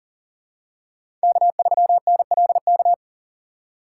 Morse code, heard as a keyed tone, about 40 wpm, 700 Hz. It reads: K3NLK
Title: USA Call Signs - 3x Character Spacing - 40wpm